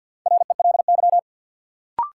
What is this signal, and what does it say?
Morse code, heard as a keyed tone, about 50 wpm, 700 Hz. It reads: RELY
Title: Sentences from Top 1200 Words - New Word Drill Alt - 50wpm